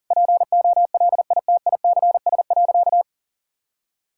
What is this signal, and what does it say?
Morse code, heard as a keyed tone, about 40 wpm, 700 Hz. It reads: POLITICS.
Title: Sentences from Top 1000 Words - Review Part 6 - 40wpm